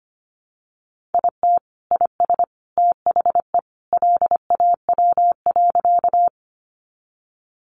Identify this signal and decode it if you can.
Morse code, heard as a keyed tone, about 25 wpm, 700 Hz. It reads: IT IS THE LAW.
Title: Sentences from Top 600 Words 25wpm